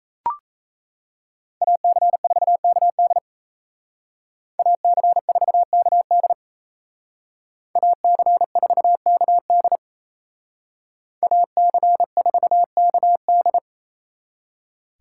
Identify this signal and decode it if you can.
Morse code, heard as a keyed tone, about 36 wpm, 700 Hz. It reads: AC4KD AC4KD AC4KD AC4KD